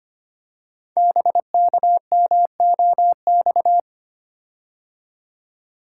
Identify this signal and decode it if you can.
Morse code, heard as a keyed tone, about 25 wpm, 700 Hz. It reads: BKMOX
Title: Code Groups - 5 Letters - 25wpm